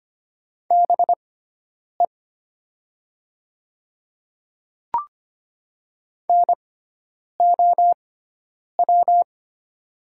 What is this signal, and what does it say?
Morse code, heard as a keyed tone, about 25 wpm, spaced as for 9 wpm, 700 Hz. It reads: BE NOW
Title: 6x Character Spacing - Top 100 Words - 25wpm